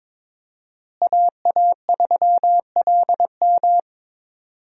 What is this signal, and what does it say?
Morse code, heard as a keyed tone, about 22 wpm, 700 Hz. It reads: AA3LM
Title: USA Call Signs - 5 Characters - Repeated 2x - 22wpm